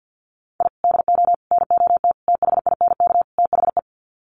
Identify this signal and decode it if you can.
Morse code, heard as a keyed tone, about 50 wpm, 700 Hz. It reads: I DO NOT THINK THE
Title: Sets of 5 Words 50wpm